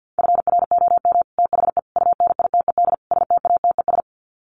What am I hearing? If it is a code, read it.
Morse code, heard as a keyed tone, about 50 wpm, 700 Hz. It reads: FROM THE UNITED STATES
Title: Sets of 4 Words 50wpm